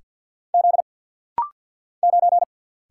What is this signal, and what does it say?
Morse code, heard as a keyed tone, about 50 wpm, 700 Hz. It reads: Z 9
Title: Single Letter-Number Alt - Rapid-Fire 50wpm